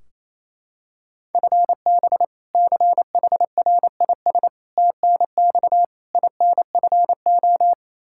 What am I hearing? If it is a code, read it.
Morse code, heard as a keyed tone, about 28 wpm, 700 Hz. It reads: FB CHRIS TNX INFO
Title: Practice QSO - Exchange 3 - More Info - 28wpm